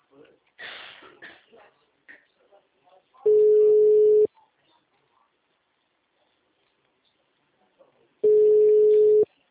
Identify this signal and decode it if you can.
Ringback tone (Europe, 425 Hz), with speech.